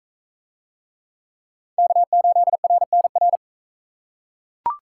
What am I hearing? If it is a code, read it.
Morse code, heard as a keyed tone, about 40 wpm, 700 Hz. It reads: K8RNR